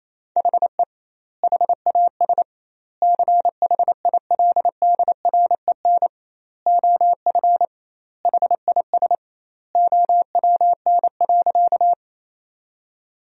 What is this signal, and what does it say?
Morse code, heard as a keyed tone, about 27 wpm, 700 Hz. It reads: HE HAS CHILDREN OF HIS OWN.
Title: 2x Word-Spacing - Sentences from Top 300 Words - 28wpm